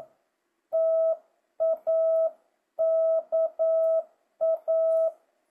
Morse code, heard as a keyed tone, about 9 wpm, 650 Hz. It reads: TAKA
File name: jh8cbh / 2025cw